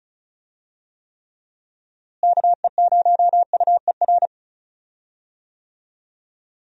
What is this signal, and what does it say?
Morse code, heard as a keyed tone, about 34 wpm, 700 Hz. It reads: KE0UER
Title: USA Call Signs - 6 Characters - Repeated 3x - 35wpm